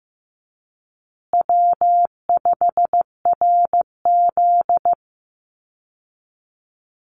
Morse code, heard as a keyed tone, about 14 wpm, 700 Hz. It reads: W5RZ
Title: Easy Call Signs Repeated 2x - 15wpm